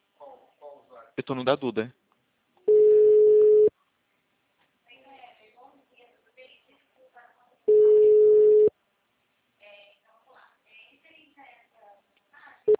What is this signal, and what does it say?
Ringback tone (Europe, 425 Hz), with speech.